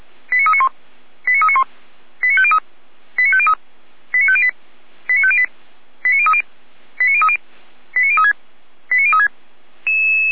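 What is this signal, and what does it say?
Five-tone selective call (ZVEI), heard as quick runs of five tones, tones of about 80 ms. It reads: 89391 89391 89503 89503 89508 89508 89030 89030 89036 89036
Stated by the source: ZVEI five-tone call received by SDR.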